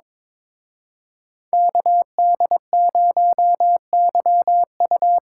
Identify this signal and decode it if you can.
Morse code, heard as a keyed tone, about 26 wpm, 700 Hz. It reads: KD0YU
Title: Speed Racing - USA 2x2 Call Signs 22wpm